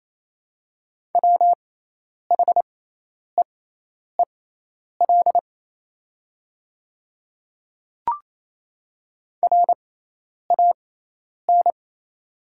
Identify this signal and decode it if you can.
Morse code, heard as a keyed tone, about 28 wpm, spaced as for 10 wpm, 690 Hz. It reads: WHEEL RAN
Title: 6x Character Spacing - Top 500 Words - 28wpm